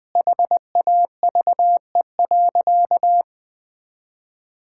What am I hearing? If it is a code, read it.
Morse code, heard as a keyed tone, about 20 wpm, 700 Hz. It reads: HAVE.
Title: Sentences from Top 1000 Words - Review Part 6 - 20wpm